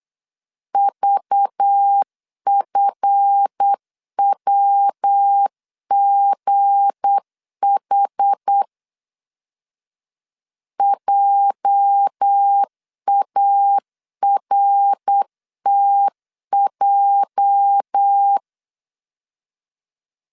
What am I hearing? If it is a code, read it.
Morse code, heard as a keyed tone, about 8 wpm, 800 Hz. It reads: VFWGH JARTJ
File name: LEZIONI / 8 YJ45 / 8 YJ 8wpm 3313